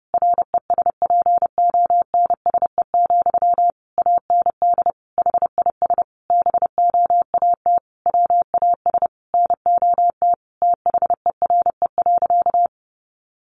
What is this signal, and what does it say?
Morse code, heard as a keyed tone, about 30 wpm, 700 Hz. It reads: RESPONSE, AND HIS BOAT WAS NOT THERE.
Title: Sentences from Top 1400 Words 30wpm